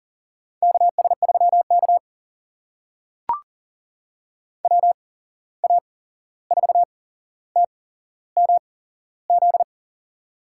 Morse code, heard as a keyed tone, about 39 wpm, spaced as for 15 wpm, 700 Hz. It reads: KS3K WA4TMZ
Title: USA Call Signs - 8x Character Spacing - 40wpm